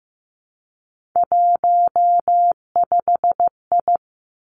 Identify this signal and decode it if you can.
Morse code, heard as a keyed tone, about 15 wpm, 700 Hz. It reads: 15I
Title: Three Challenging Letter-Number Combinations 15wpm